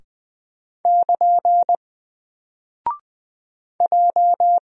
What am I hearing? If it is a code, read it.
Morse code, heard as a keyed tone, about 20 wpm, 700 Hz. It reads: <KN> J
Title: Single Character (Letter:Number:Punctuation:Prosign) 20wpm